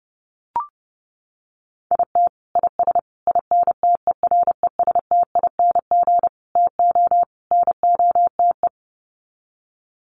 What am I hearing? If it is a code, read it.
Morse code, heard as a keyed tone, about 30 wpm, 700 Hz. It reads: IT IS INTERESTING TO NOTE
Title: Sets of 5 Words Encore2 30wpm